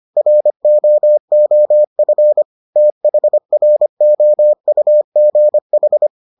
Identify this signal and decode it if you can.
Morse code, heard as a keyed tone, about 25 wpm, 600 Hz. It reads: ROOF THROUGH